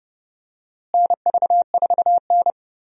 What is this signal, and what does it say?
Morse code, heard as a keyed tone, about 30 wpm, 700 Hz. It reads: NV4N
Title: USA Call Signs - 4 Characters - Repeated 2x - 30wpm